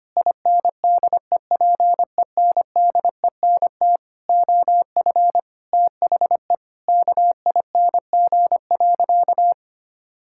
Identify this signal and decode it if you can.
Morse code, heard as a keyed tone, about 25 wpm, 700 Hz. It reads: INDEPENDENT OF THE KING.
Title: Sentences from Top 1000 Words - Review Part 6 - 25wpm